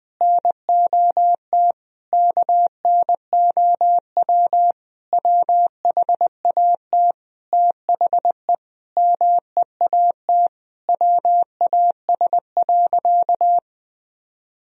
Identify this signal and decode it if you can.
Morse code, heard as a keyed tone, about 20 wpm, 700 Hz. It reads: NOT KNOW WHAT THE MEAT WAS.
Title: Sentences from Top 1000 Words - Review Part 3 - 20wpm